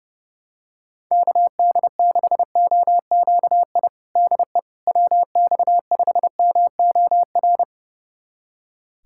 Morse code, heard as a keyed tone, about 30 wpm, 700 Hz. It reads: KD6OQI DE WX5MOR
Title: USA Callsign DE Callsign - 6 Characters - 30wpm